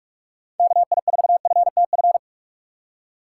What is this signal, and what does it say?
Morse code, heard as a keyed tone, about 46 wpm, 700 Hz. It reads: KI4FTF
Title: USA Call Signs - 6 Characters - 6x Character Spacing - 45wpm